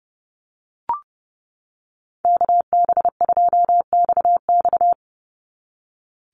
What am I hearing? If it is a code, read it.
Morse code, heard as a keyed tone, about 30 wpm, 700 Hz. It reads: KB2XX
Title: NA SOTA Chaser Callsigns 30wpm